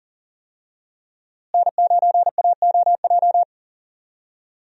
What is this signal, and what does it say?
Morse code, heard as a keyed tone, about 40 wpm, 700 Hz. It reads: N9AOJ
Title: USA Call Signs - 5 Characters - Repeated 3x - 40wpm